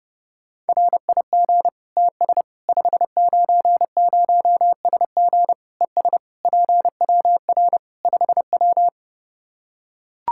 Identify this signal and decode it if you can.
Morse code, heard as a keyed tone, about 30 wpm, 700 Hz. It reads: RIG TS 590SG ES PWR 5W
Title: Practice QSO - Exchange 2 - Info - 30wpm